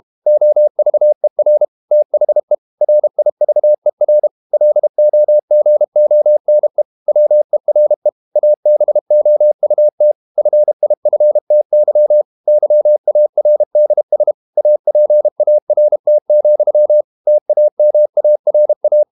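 Morse code, heard as a keyed tone, about 32 wpm, 600 Hz. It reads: OVER THE RIVER LOGONE WERE ABOUT FIFTY YARDS APART, TAMARA
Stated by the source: Morse code practice at 32 wpm from CW med Gnister, reading Ken Follett's novel Never in English.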